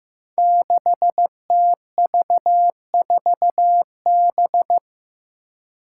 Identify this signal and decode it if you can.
Morse code, heard as a keyed tone, about 15 wpm, 700 Hz. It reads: 6TV4B
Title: Five Challenging Letter-Number Combinations 15wpm